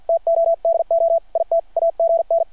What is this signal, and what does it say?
Morse code, heard as a keyed tone, about 50 wpm, 670 Hz. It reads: TO DO IT AGN
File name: archived / CRSnet